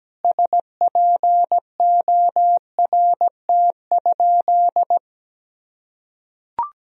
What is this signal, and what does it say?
Morse code, heard as a keyed tone, about 17 wpm, 700 Hz. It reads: SPORT?
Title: Sentences from Top 1000 Words - Review Part 4 - 17wpm